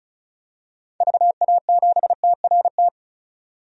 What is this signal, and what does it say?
Morse code, heard as a keyed tone, about 34 wpm, 700 Hz. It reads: VA7TRT
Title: DXCC-100 Call Signs - 6 Characters - 35wpm